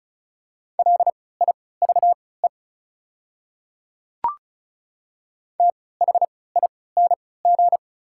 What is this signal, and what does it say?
Morse code, heard as a keyed tone, about 34 wpm, spaced as for 20 wpm, 700 Hz. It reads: LIVE THING
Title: 3x Character Spacing - Top 200 Words - 35wpm